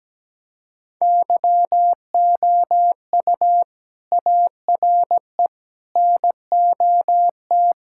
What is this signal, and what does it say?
Morse code, heard as a keyed tone, about 17 wpm, 700 Hz. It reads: YOU ARE NOT
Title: Sentences from Top 1600 Words 17wpm